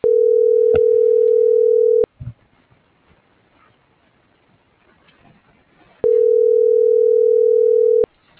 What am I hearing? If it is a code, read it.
Ringback tone (North America), 440 + 480 Hz.